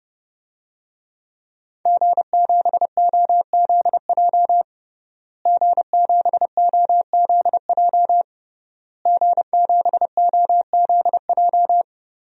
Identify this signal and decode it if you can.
Morse code, heard as a keyed tone, about 32 wpm, 700 Hz. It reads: G7OZJ G7OZJ G7OZJ
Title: DXCC-100 Call Signs - 5 Characters - Repeated 3x - 30wpm